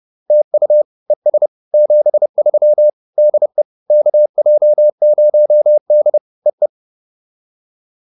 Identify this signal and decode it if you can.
Morse code, heard as a keyed tone, about 30 wpm, 600 Hz. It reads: TU ES 73 DE KJ0D EE